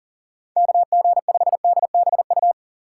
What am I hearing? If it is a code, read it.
Morse code, heard as a keyed tone, about 40 wpm, 700 Hz. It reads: KG5DBU
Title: USA Call Signs - 6 Characters - 2x Character Spacing - 40wpm